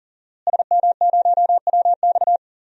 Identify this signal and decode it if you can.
Morse code, heard as a keyed tone, about 39 wpm, 700 Hz. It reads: SM0WX
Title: DXCC-100 Call Signs - 5 Characters - Repeated 2x - 40wpm